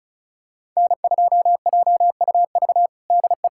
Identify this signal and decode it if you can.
Morse code, heard as a keyed tone, about 34 wpm, 700 Hz. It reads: N2JUV DE
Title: USA Callsign DE Callsign - 5 Characters - 35wpm